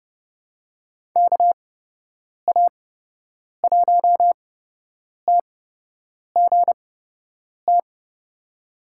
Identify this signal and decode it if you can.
Morse code, heard as a keyed tone, about 30 wpm, spaced as for 8 wpm, 700 Hz. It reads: KA1TGT
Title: USA Call Signs - 8x Character Spacing - 30wpm